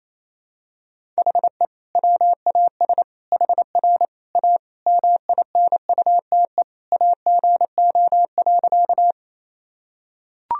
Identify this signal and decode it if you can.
Morse code, heard as a keyed tone, about 28 wpm, 700 Hz. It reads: HE WAS HR A MINUTE AGO.
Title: Sentences from Top 500 Words and Abbreviations 28wpm